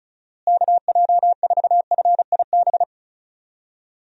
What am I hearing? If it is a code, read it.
Morse code, heard as a keyed tone, about 36 wpm, 700 Hz. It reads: KJ4FIB
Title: USA Call Signs - 6 Characters - Repeated 3x - 35wpm